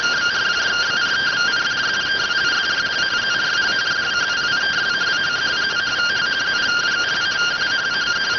Two-tone teleprinter signal, about 170 Hz shift.